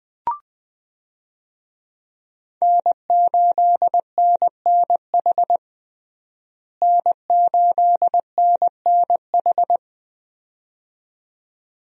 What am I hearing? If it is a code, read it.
Morse code, heard as a keyed tone, about 20 wpm, 700 Hz. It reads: N8NNH N8NNH
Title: USA Call Signs - 5 Characters - Repeated 2x - 20wpm